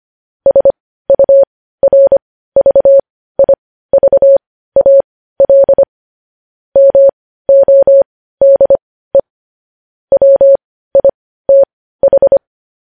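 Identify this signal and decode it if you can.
Morse code, heard as a keyed tone, about 25 wpm, spaced as for 15 wpm, 550 Hz. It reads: SURVIVAL MODE WITH
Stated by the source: Morse news headlines sent at 15 wpm.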